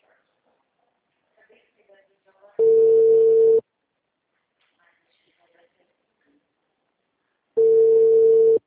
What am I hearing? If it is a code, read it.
Ringback tone (China, 450 Hz), with speech.